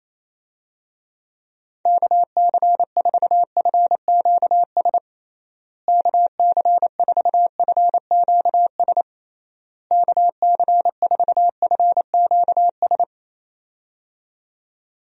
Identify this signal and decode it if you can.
Morse code, heard as a keyed tone, about 28 wpm, 700 Hz. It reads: KC4FQS KC4FQS KC4FQS
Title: USA Call Signs - 6 Characters - Repeated 3x - 28wpm